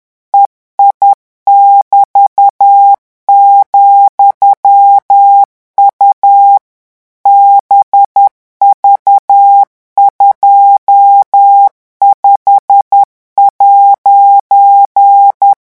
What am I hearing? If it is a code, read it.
Morse code, heard as a keyed tone, about 10 wpm, 800 Hz. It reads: EI<BT>,U BV25'
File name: morse mp3 / revision13-vitesse 10 mots